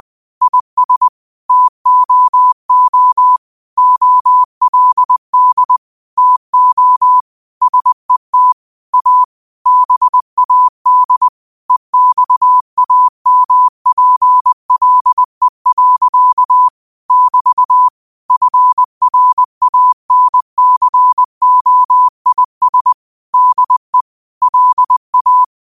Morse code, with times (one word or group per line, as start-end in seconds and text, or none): 0.4-1.1 IS
1.5-3.4 TOO
3.8-5.8 OLD
6.2-7.2 TO
7.6-8.5 SET
8.9-9.2 A
9.7-11.3 BAD
11.7-16.7 EXAMPLE.
17.1-17.9 <BT>
18.3-22.9 FRANCOIS
23.3-24.0 DE
24.4-25.4 LA